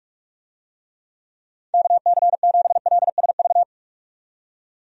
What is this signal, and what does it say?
Morse code, heard as a keyed tone, about 47 wpm, 700 Hz. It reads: KC7LSV
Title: USA Call Signs - 6 Characters - 45wpm